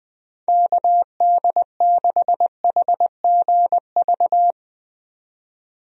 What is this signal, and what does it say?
Morse code, heard as a keyed tone, about 21 wpm, 700 Hz. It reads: KD6HGV
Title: USA Call Signs - 6 Characters - 20wpm